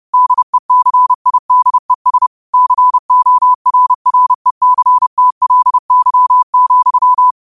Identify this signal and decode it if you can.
Morse code, heard as a keyed tone, about 30 wpm, 1000 Hz. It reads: DECIDES CORRECTLY,
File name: qotdfiles / 30wpm